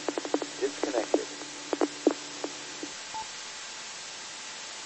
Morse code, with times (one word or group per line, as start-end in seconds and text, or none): none